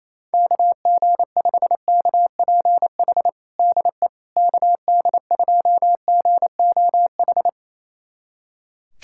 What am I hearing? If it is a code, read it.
Morse code, heard as a keyed tone, about 28 wpm, 700 Hz. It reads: KG5KPH DE KD2GOH
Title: USA Callsign DE Callsign 28wpm